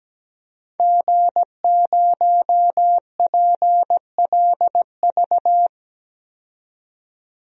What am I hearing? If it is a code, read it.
Morse code, heard as a keyed tone, about 17 wpm, 700 Hz. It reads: G0PLV
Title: DXCC-100 Call Signs - 5 Characters - 17wpm